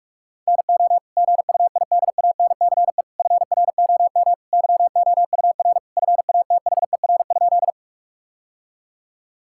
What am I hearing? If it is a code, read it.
Morse code, heard as a keyed tone, about 42 wpm, 700 Hz. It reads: NO GUIDANCE FROM YOUR FATHER?
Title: Sentences from Top 1600 Words 45wpm